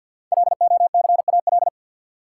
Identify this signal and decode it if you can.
Morse code, heard as a keyed tone, about 51 wpm, 700 Hz. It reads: LOCAL
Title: Sentences from Top 1000 Words - Review Part 3 - New Word Drill - 50wpm